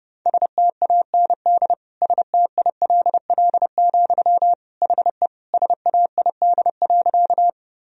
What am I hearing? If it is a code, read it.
Morse code, heard as a keyed tone, about 32 wpm, 700 Hz. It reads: STAND STILL, HE SAID.
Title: Sentences from Top 1000 Words - Review Part 1 - 30wpm